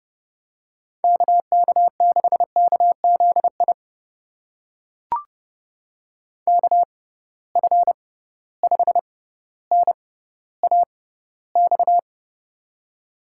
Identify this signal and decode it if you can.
Morse code, heard as a keyed tone, about 30 wpm, spaced as for 15 wpm, 700 Hz. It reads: KK6KZI KF5NAX
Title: USA Call Signs - 6 Characters - 6x Character Spacing - 30wpm